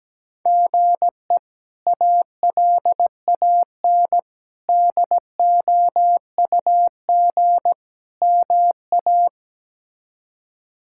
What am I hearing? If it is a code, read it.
Morse code, heard as a keyed tone, about 17 wpm, 700 Hz. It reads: GE ALAN DOUG MA
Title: K1USN SST QSO Practice - 17wpm